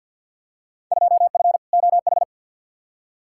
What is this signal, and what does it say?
Morse code, heard as a keyed tone, about 47 wpm, 700 Hz. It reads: 2F OH
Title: Field Day Exchanges - 50wpm